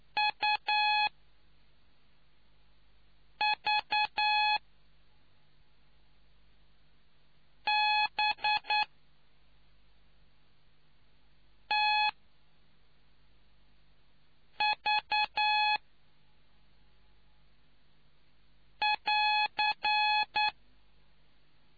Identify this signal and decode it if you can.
Morse code, heard as a keyed tone, about 9 wpm, spaced as for 3 wpm, 850 Hz. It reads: UVBTV<AR>